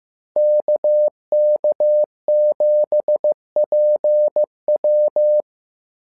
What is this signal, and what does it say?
Morse code, heard as a keyed tone, about 15 wpm, 600 Hz. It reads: KK7PW
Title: SOTA QSO Practice 15wpm